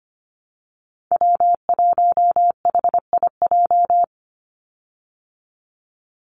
Morse code, heard as a keyed tone, about 25 wpm, 700 Hz. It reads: W1HIJ
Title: Call Signs Repeated 2x - 25wpm